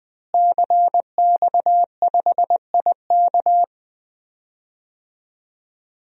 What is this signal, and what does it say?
Morse code, heard as a keyed tone, about 20 wpm, 700 Hz. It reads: CX5IK
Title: DXCC-100 Call Signs - 5 Characters - Repeated 2x - 20wpm